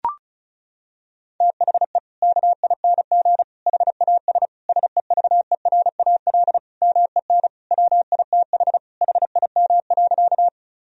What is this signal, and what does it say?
Morse code, heard as a keyed tone, about 34 wpm, 700 Hz. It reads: THE KING HAS SEVERAL MEN WITH HIM.
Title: Sentences from Top 1000 Words - Review Part 2 - 35wpm